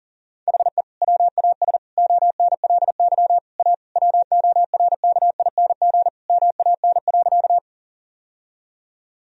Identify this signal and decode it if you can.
Morse code, heard as a keyed tone, about 37 wpm, 700 Hz. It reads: HE WAS ONLY A WORKING MAN.
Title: Sentences from Top 1000 Words - Review Part 4 - 40wpm